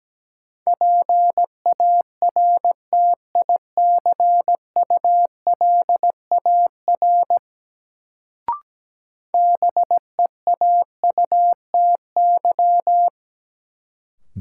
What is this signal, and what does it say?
Morse code, heard as a keyed tone, about 17 wpm, 700 Hz. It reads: PARTICULAR BEAUTY
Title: Sentences from Top 600 Words - New Word Drill - 17wpm